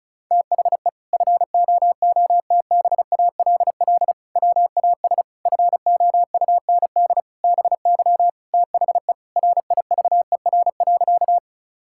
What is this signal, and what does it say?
Morse code, heard as a keyed tone, about 36 wpm, 700 Hz. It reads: THE FOOTBALL WAS FOUND BY THE RIVER.
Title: Sentences from Top 1600 Words 35wpm